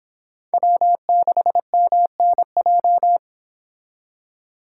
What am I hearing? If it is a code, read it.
Morse code, heard as a keyed tone, about 25 wpm, 700 Hz. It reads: W6MNJ